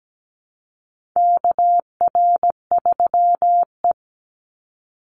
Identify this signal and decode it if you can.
Morse code, heard as a keyed tone, about 17 wpm, 700 Hz. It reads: KR3E
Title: Call Signs Repeated 2x - 17wpm